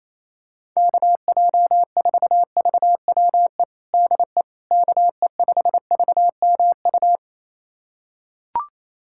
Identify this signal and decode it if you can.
Morse code, heard as a keyed tone, about 28 wpm, 700 Hz. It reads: KJ4VWE DE KE5VMU
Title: USA Callsign DE Callsign - 6 Characters - 28wpm